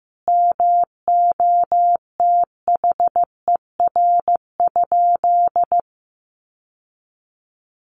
Morse code, heard as a keyed tone, about 15 wpm, 700 Hz. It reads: MOTHER?
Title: Sentences from Top 1000 Words 15wpm